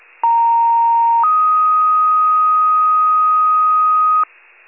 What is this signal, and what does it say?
Two-tone page, heard as a one-second tone, then a three-second tone, about 930 then 1250 Hz. It is heard from 0.2 to 4.2 s.